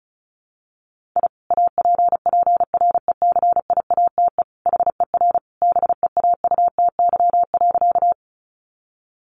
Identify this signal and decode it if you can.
Morse code, heard as a keyed tone, about 34 wpm, 700 Hz. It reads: I APPRECIATE HER BEAUTY.
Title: Sentences from Top 1300 Words 35wpm